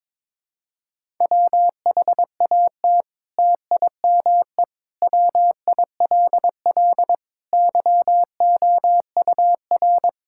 Morse code, heard as a keyed tone, about 22 wpm, 700 Hz. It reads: WHAT TIME WILL YOUR
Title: Sentences from Top 1000 Words - Review Part 5 - 22wpm